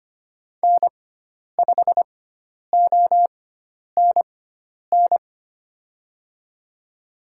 Morse code, heard as a keyed tone, about 25 wpm, spaced as for 10 wpm, 700 Hz. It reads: N5ONN
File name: Lesson 16g - Letter c - Call Signs 25wpm